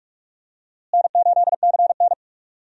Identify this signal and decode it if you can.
Morse code, heard as a keyed tone, about 45 wpm, 700 Hz. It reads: N8CN
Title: USA Call Signs - 4 Characters - Repeated 2x - 45wpm